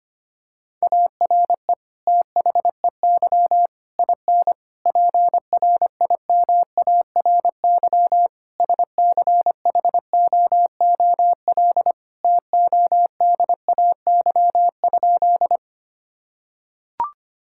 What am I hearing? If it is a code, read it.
Morse code, heard as a keyed tone, about 25 wpm, 700 Hz. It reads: ARE THEY IN PRIMARY SCHOOL TODAY?
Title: Sentences from Top 1000 Words - Review Part 5 - 25wpm